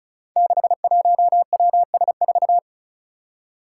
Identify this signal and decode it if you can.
Morse code, heard as a keyed tone, about 34 wpm, 700 Hz. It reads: 61WS4
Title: Five Challenging Letter-Number Combinations 35wpm